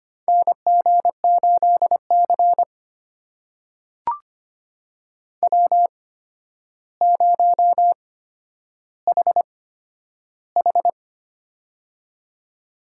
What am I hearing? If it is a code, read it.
Morse code, heard as a keyed tone, about 25 wpm, spaced as for 11 wpm, 700 Hz. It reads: NG8C W0HH
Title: USA Call Signs - 4 Characters - 8x Character Spacing - 25wpm